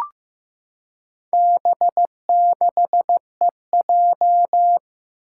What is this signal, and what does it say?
Morse code, heard as a keyed tone, about 15 wpm, 700 Hz. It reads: B6EJ
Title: Four Challenging Letter-Number Combinations 15wpm